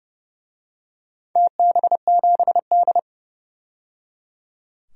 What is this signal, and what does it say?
Morse code, heard as a keyed tone, about 31 wpm, 700 Hz. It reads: TB7D
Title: DXCC-100 Call Signs - 4 Characters - 30wpm